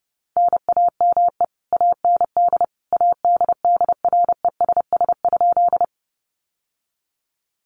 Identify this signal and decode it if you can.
Morse code, heard as a keyed tone, about 30 wpm, 700 Hz. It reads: NAME AND ADDRESS?
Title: Sentences from Top 1000 Words 30wpm